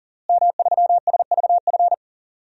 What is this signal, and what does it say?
Morse code, heard as a keyed tone, about 40 wpm, 700 Hz. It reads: M3SVF
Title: DXCC-100 Call Signs - 5 Characters - Repeated 2x - 40wpm